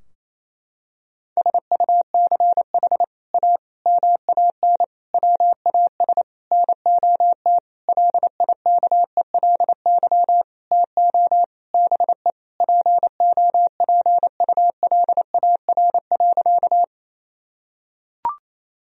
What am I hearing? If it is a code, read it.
Morse code, heard as a keyed tone, about 28 wpm, 700 Hz. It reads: SUCH A MAN WAS NOT LIKELY TO BE POPULAR.
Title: Sentences from Top 1000 Words - Review Part 4 - 28wpm